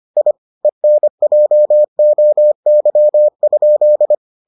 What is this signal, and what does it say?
Morse code, heard as a keyed tone, about 25 wpm, 600 Hz. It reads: I ENJOY?